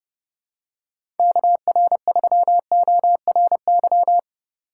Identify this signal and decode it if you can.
Morse code, heard as a keyed tone, about 30 wpm, 700 Hz. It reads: KR3ORY
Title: USA Call Signs - 6x Character Spacing - 30wpm